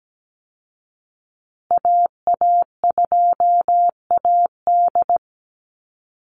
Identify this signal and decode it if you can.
Morse code, heard as a keyed tone, about 17 wpm, 700 Hz. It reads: AA2AD
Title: Call Signs Repeated 2x - 17wpm